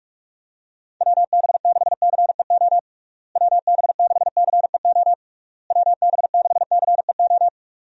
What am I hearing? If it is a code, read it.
Morse code, heard as a keyed tone, about 46 wpm, 700 Hz. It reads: WB6CEO WB6CEO WB6CEO
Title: USA Call Signs - 6 Characters - Repeated 3x - 45wpm